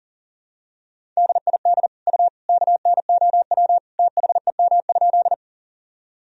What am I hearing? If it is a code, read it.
Morse code, heard as a keyed tone, about 37 wpm, 700 Hz. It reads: DID U KNOW THEM?
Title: Sentences from Top 100 Words and Abbreviations 40wpm